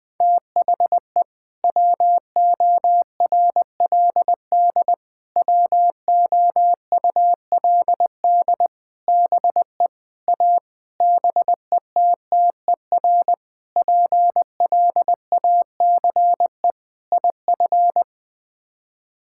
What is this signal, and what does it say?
Morse code, heard as a keyed tone, about 20 wpm, 700 Hz. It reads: THE WORLD WOULD BE A BETTER PLACE IF
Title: Sets of 8 Words - 20wpm